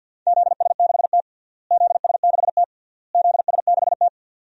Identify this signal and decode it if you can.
Morse code, heard as a keyed tone, about 52 wpm, 700 Hz. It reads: ZS6T ZS6T ZS6T
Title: DXCC-100 Call Signs - 4 Characters - Repeated 3x - 50wpm